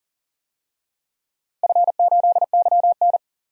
Morse code, heard as a keyed tone, about 44 wpm, 700 Hz. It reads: F8YN
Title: DXCC-100 Call Signs - 4 Characters - 40wpm